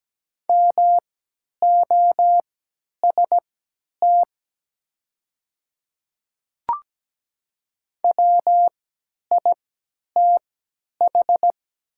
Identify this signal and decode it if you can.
Morse code, heard as a keyed tone, about 17 wpm, spaced as for 10 wpm, 700 Hz. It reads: MOST WITH
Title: 3x Character Spacing - Top 100 Words - 17wpm